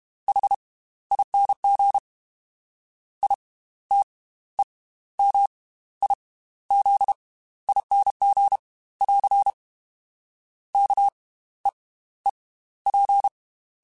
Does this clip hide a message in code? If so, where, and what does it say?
Yes: Morse code from 0.3 to 13.3 s, heard as a keyed tone, about 32 wpm, spaced as for 15 wpm, 800 Hz. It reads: HING ITEMIZING <AR> KEEP